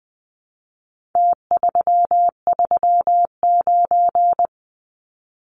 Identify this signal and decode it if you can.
Morse code, heard as a keyed tone, about 20 wpm, 700 Hz. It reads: T339
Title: Numbers - Contest Style - 4-Digit 20wpm